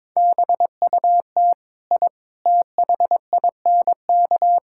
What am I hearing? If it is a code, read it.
Morse code, heard as a keyed tone, about 22 wpm, 700 Hz. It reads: BUT I THINK
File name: practice / 22wpm / Sentences with 4-Word Phrases - 22wpm